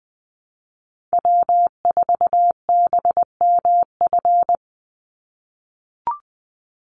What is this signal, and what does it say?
Morse code, heard as a keyed tone, about 20 wpm, 700 Hz. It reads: W4BMF
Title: USA Call Signs 20wpm